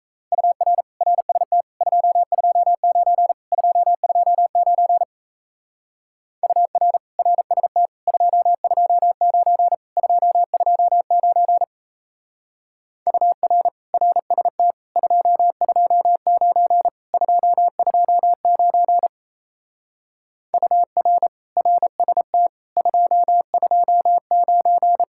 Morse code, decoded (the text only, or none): UR RST 229 229 UR RST 229 229 UR RST 229 229 UR RST 229